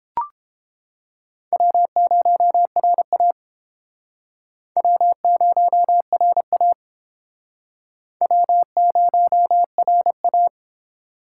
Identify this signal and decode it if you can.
Morse code, heard as a keyed tone, about 32 wpm, 700 Hz. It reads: W0RA W0RA W0RA